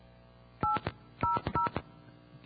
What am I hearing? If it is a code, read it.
DTMF, heard as telephone touch tones, tones of about 140 ms. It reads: 800